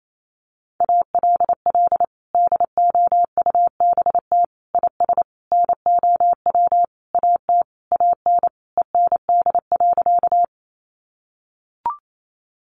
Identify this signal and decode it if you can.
Morse code, heard as a keyed tone, about 28 wpm, 700 Hz. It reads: ALL DOUBT IS NOW AT AN END.
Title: Sentences from Top 1500 Words 28wpm